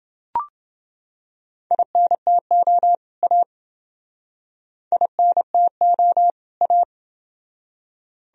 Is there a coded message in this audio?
Yes: Morse code from 1.7 to 6.8 s, heard as a keyed tone, about 29 wpm, 700 Hz. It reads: INTO A INTO A